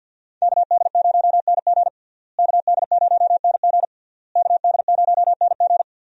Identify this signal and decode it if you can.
Morse code, heard as a keyed tone, about 52 wpm, 700 Hz. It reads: KD0NG KD0NG KD0NG
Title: USA Call Signs - 5 Characters - Repeated 3x - 50wpm